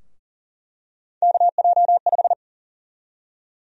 Morse code, heard as a keyed tone, about 40 wpm, 700 Hz. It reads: KJ5
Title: USA Call Sign Prefixes - 40wpm